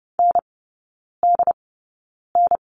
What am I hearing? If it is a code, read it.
Morse code, heard as a keyed tone, about 30 wpm, spaced as for 9 wpm, 700 Hz. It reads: NDN
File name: course / 30:10wpm / 087 Lesson 15e - Number 5 - Three Characters 30wpm